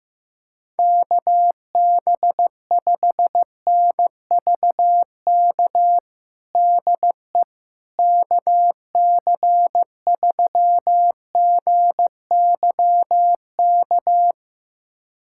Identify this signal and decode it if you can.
Morse code, heard as a keyed tone, about 16 wpm, 700 Hz. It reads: KB5NVK DE KC3GYK
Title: USA Callsign DE Callsign 15wpm